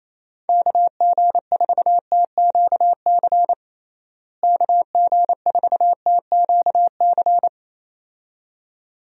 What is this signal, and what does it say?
Morse code, heard as a keyed tone, about 27 wpm, 700 Hz. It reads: KG4TQC KG4TQC
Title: USA Call Signs - 6 Characters - Repeated 2x - 28wpm